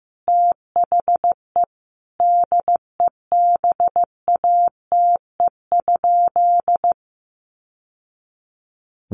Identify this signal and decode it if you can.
Morse code, heard as a keyed tone, about 15 wpm, 700 Hz. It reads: THE DEBATE?
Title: Sentences from Top 1100 Words 15wpm